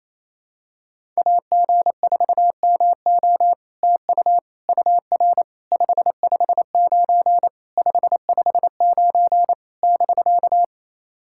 Morse code, heard as a keyed tone, about 28 wpm, 700 Hz. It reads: AG4MO TU UR 559 559 <BK>
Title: SOTA QSO Practice 28wpm